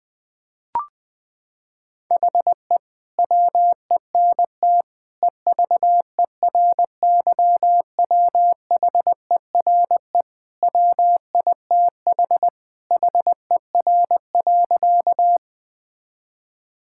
Morse code, heard as a keyed tone, about 20 wpm, 700 Hz. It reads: HE WENT EVERYWHERE WITH HER.
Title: Sentences from Top 1600 Words 20wpm